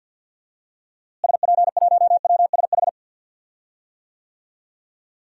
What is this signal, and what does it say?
Morse code, heard as a keyed tone, about 47 wpm, 700 Hz. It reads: SP1WSH
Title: DXCC-100 Call Signs - 6 Characters - Repeated 3x - 50wpm